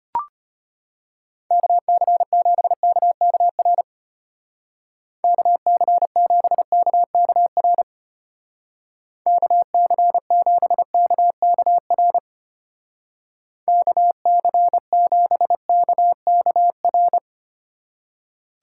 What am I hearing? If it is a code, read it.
Morse code, heard as a keyed tone, about 31 wpm, 700 Hz. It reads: KC7KKR KC7KKR KC7KKR KC7KKR
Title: Speed Racing - USA 2x3 Call Signs 25wpm